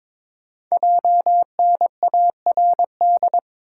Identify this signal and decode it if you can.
Morse code, heard as a keyed tone, about 22 wpm, 700 Hz. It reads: JNARD
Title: Code Groups - 5 Letters - 22wpm